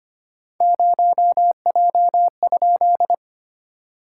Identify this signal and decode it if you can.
Morse code, heard as a keyed tone, about 25 wpm, 700 Hz. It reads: 0J?
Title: SOTA QSO Practice 25wpm